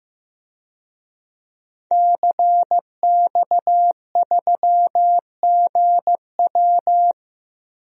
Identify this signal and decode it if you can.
Morse code, heard as a keyed tone, about 15 wpm, 700 Hz. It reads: CX3GW
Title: DXCC-100 Call Signs - 5 Characters - Repeated 2x - 15wpm